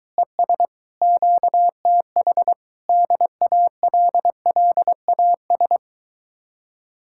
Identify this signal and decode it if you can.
Morse code, heard as a keyed tone, about 23 wpm, 700 Hz. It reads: ES QTH DALLAS